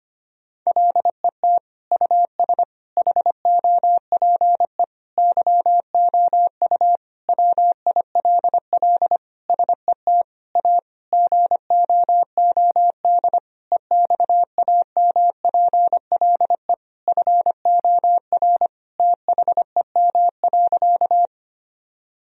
Morse code, heard as a keyed tone, about 25 wpm, 700 Hz. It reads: LET US HOPE YOU WILL SET A GOOD EXAMPLE FOR THEM.
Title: Sentences from Top 1000 Words - Review Part 1 - 25wpm